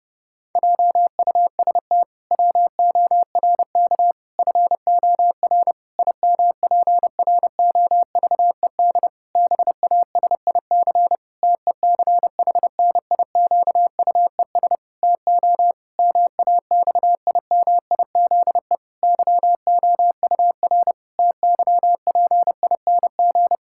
Morse code, heard as a keyed tone, about 32 wpm, 700 Hz. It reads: JUST WORK FOR IMPROVED BASIC TECHNIQUES TO MAXIMIZE YOUR TYPING
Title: Pangram Challenge v2 - 30wpm